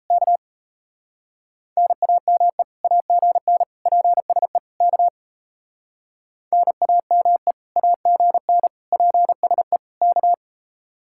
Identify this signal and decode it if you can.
Morse code, heard as a keyed tone, about 34 wpm, 700 Hz. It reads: K NAME AGN PSE K NAME AGN PSE K
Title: Speed-Racing Common QSO Phrases Encore - 28wpm